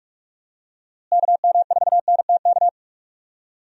Morse code, heard as a keyed tone, about 46 wpm, 700 Hz. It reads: KM4NTK
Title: USA Call Signs - 6 Characters - Repeated 2x - 45wpm